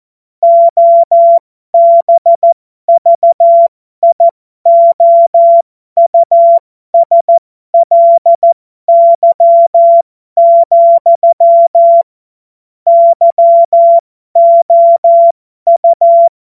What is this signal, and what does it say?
Morse code, heard as a keyed tone, about 14 wpm, spaced as for 12 wpm, 680 Hz. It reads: OBVIOUSLY, YOU